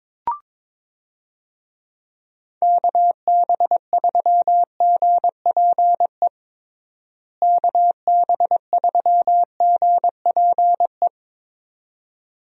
Morse code, heard as a keyed tone, about 22 wpm, 700 Hz. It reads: KB3GPE KB3GPE
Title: USA Call Signs - 6 Characters - Repeated 2x - 22wpm